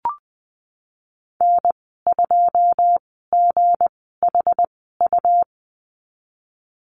Morse code, heard as a keyed tone, about 20 wpm, spaced as for 15 wpm, 700 Hz. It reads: N2GHU
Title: 167 Lesson 27g - Number 7 - Call Signs 20wpm